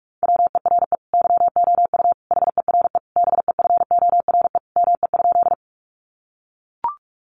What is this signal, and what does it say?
Morse code, heard as a keyed tone, about 44 wpm, 700 Hz. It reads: WERE YOU HERE BEFORE ME?
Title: Sentences from Top 200 Words 45wpm